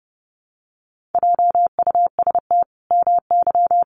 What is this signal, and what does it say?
Morse code, heard as a keyed tone, about 30 wpm, 700 Hz. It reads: JUST MY
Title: Sentences from Top 1200 Words 30wpm